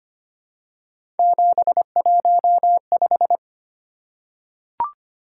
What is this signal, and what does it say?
Morse code, heard as a keyed tone, about 25 wpm, 700 Hz. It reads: 715
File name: Three Letter-Number Combinations - 25wpm